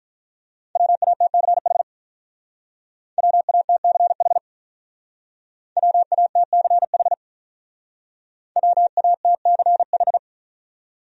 Morse code, heard as a keyed tone, about 46 wpm, 700 Hz. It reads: WATCH WATCH WATCH WATCH